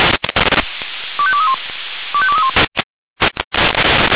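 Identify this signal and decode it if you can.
Five-tone selective call (ZVEI), heard as quick runs of five tones, tones of about 70 ms. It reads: 25231 25231
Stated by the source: ZVEI five-tone selective call received with an SDR.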